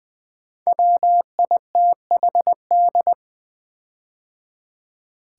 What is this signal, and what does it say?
Morse code, heard as a keyed tone, about 20 wpm, 700 Hz. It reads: WITHD
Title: Build A Word - 5 Letters At A Time - 10-Letter Words - 20wpm